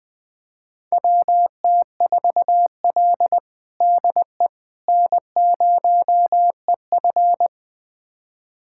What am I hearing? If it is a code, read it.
Morse code, heard as a keyed tone, about 20 wpm, 700 Hz. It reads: WT4L DE N0EF
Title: USA Callsign DE Callsign - 4 Characters - 20wpm